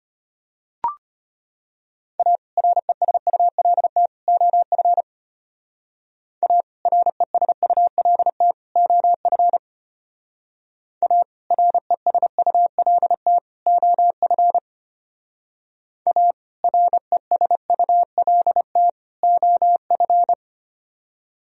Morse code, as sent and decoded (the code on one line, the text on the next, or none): .- / .-. ... ..- .-.. - / --- ..-. / .- / .-. ... ..- .-.. - / --- ..-. / .- / .-. ... ..- .-.. - / --- ..-. / .- / .-. ... ..- .-.. - / --- ..-.
A RESULT OF A RESULT OF A RESULT OF A RESULT OF